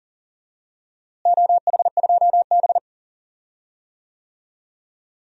Morse code, heard as a keyed tone, about 40 wpm, 690 Hz. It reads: OH2B
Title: DXCC-100 Call Signs - 4 Characters - Repeated 2x - 40wpm